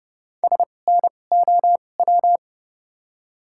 Morse code, heard as a keyed tone, about 30 wpm, spaced as for 22 wpm, 690 Hz. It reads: SNOW
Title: 2x Character Spacing - Top 500 Words - 30wpm